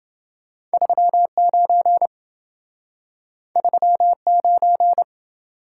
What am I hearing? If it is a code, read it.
Morse code, heard as a keyed tone, about 27 wpm, 700 Hz. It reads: 39 39